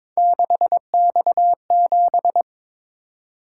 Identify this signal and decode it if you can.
Morse code, heard as a keyed tone, about 26 wpm, 700 Hz. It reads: 6X7
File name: Speed-Racing - Three Letter-Number Combinations - 22wpm